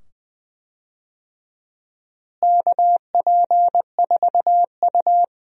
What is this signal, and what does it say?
Morse code, heard as a keyed tone, about 20 wpm, 700 Hz. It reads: KP4U
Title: DXCC-100 Call Signs - 4 Characters - Repeated 2x - 20wpm